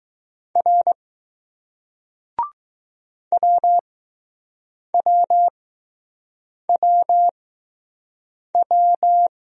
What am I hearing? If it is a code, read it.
Morse code, heard as a keyed tone, about 19 wpm, spaced as for 6 wpm, 700 Hz. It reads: R WWWW